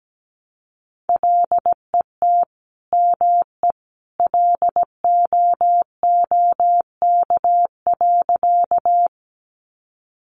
Morse code, heard as a keyed tone, about 17 wpm, 700 Hz. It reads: LET ME LOOK.
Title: Sentences from Top 300 Words 17wpm